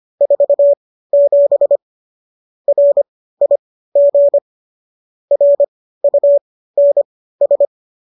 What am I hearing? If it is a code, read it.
Morse code, heard as a keyed tone, about 25 wpm, spaced as for 15 wpm, 570 Hz. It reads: 47 RIG RUNS